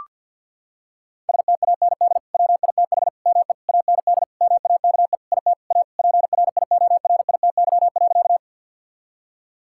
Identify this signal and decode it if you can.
Morse code, heard as a keyed tone, about 50 wpm, 700 Hz. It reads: STAND WITH ME AND MAKE IT A PRIORITY.
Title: Sentences from Top 1600 Words 50wpm